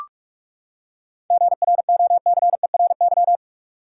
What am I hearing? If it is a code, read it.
Morse code, heard as a keyed tone, about 46 wpm, 700 Hz. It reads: GROCERY
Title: Seven Letter Words - 45wpm